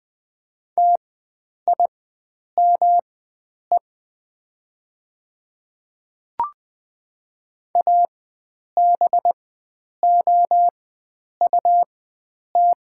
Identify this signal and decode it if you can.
Morse code, heard as a keyed tone, about 20 wpm, spaced as for 9 wpm, 700 Hz. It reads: TIME ABOUT
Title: 4x Character Spacing - Top 100 Words - 20wpm